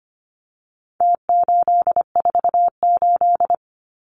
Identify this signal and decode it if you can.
Morse code, heard as a keyed tone, about 25 wpm, 700 Hz. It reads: T848
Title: Numbers - Contest Style - 4-Digit 25wpm